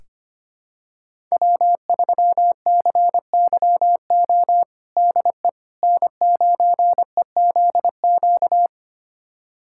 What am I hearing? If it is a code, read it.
Morse code, heard as a keyed tone, about 25 wpm, 700 Hz. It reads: W3CYO DE N9EZQ
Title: USA Callsign DE Callsign - 5 Characters - 25wpm